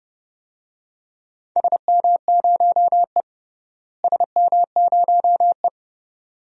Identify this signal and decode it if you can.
Morse code, heard as a keyed tone, about 30 wpm, 700 Hz. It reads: SM0E SM0E
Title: DXCC-100 Call Signs - 4 Characters - Repeated 2x - 30wpm